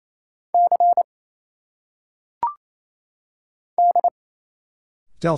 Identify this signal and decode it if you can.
Morse code, heard as a keyed tone, about 29 wpm, spaced as for 7 wpm, 700 Hz. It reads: C DE
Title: Single Letters - Phonetic Pronunciation 28wpm